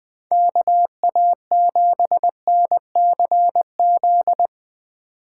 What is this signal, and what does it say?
Morse code, heard as a keyed tone, about 22 wpm, 700 Hz. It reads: KA7NCZ
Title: Speed Racing - USA 2x3 Call Signs 20wpm